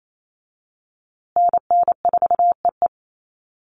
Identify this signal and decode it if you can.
Morse code, heard as a keyed tone, about 28 wpm, 700 Hz. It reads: NN4EE
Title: USA Call Signs Repeated 2x 28wpm